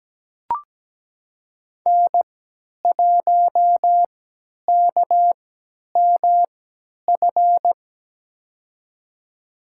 Morse code, heard as a keyed tone, about 17 wpm, spaced as for 10 wpm, 700 Hz. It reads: N1KMF
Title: USA Call Signs - 3x Character Spacing - 17wpm